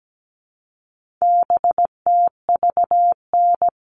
Morse code, heard as a keyed tone, about 17 wpm, 700 Hz. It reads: BTVN
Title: All Cut Numbers - 4-Digit 17wpm